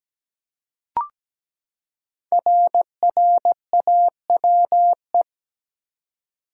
Morse code, heard as a keyed tone, about 17 wpm, 700 Hz. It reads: RRAWE
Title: Code Groups - 5 Letters - 17wpm